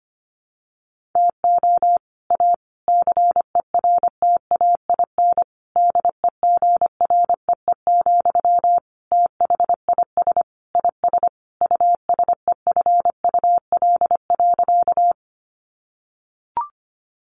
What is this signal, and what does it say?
Morse code, heard as a keyed tone, about 25 wpm, 700 Hz. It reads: TO A CERTAIN DEGREE, THIS IS USEFUL.
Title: Sentences from Top 1500 Words 25wpm